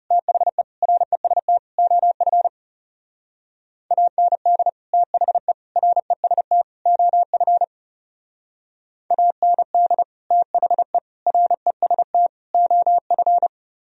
Morse code, heard as a keyed tone, about 36 wpm, 700 Hz. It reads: THE REST OF AND THE REST OF AND THE REST OF
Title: Speed-Racing - Sets of 4 Words 30wpm